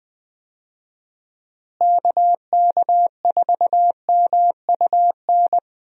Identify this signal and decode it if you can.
Morse code, heard as a keyed tone, about 20 wpm, 700 Hz. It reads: KK4MUN
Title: USA Call Signs - 6 Characters - Repeated 2x - 20wpm